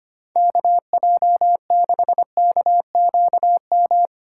Morse code, heard as a keyed tone, about 25 wpm, 700 Hz. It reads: KJ6KQM
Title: USA Call Signs - 8x Character Spacing - 25wpm